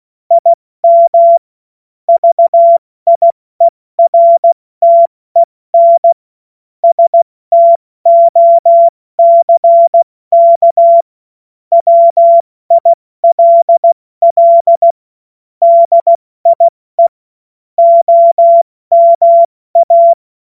Morse code, heard as a keyed tone, about 16 wpm, 680 Hz. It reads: IM VIERTEN STOCK WILL DIE OMA
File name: Morsekurs / Witze / Witz 31. August 2021, 70 F80